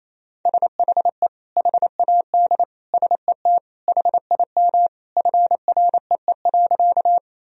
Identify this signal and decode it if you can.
Morse code, heard as a keyed tone, about 28 wpm, 700 Hz. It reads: SHE HAD SET HIM FREE.
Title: Sentences from Top 1000 Words - Review Part 1 - 28wpm